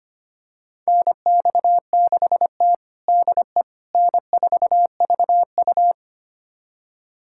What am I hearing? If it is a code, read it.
Morse code, heard as a keyed tone, about 25 wpm, 700 Hz. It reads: NX6T DE N4VU
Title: USA Callsign DE Callsign - 4 Characters - 25wpm